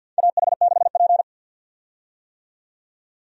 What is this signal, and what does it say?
Morse code, heard as a keyed tone, about 50 wpm, 700 Hz. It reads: AH6P
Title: USA Call Signs - 4 Characters - 50wpm